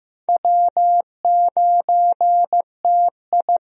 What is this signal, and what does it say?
Morse code, heard as a keyed tone, about 15 wpm, 700 Hz. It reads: W9TI
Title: USA Call Signs - 4 Characters - Repeated 3x - 15wpm